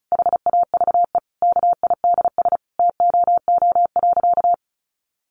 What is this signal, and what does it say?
Morse code, heard as a keyed tone, about 34 wpm, 700 Hz. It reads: HAVE KIDS TOO.
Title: Easy Phrases 35wpm